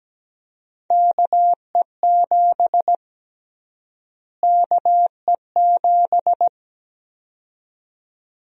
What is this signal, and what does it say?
Morse code, heard as a keyed tone, about 17 wpm, 700 Hz. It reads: KE7 KE7
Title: USA Call Sign Prefixes - Repeated 2x - 17wpm